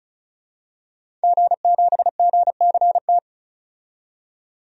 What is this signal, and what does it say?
Morse code, heard as a keyed tone, about 36 wpm, 700 Hz. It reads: G7GCT
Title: DXCC-100 Call Signs - 5 Characters - 35wpm